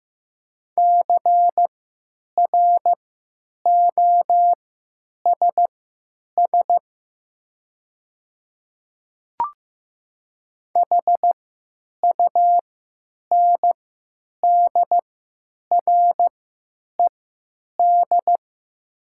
Morse code, heard as a keyed tone, about 15 wpm, spaced as for 9 wpm, 700 Hz. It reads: CROSS HUNDRED
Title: 3x Character Spacing - Top 500 Words - 15wpm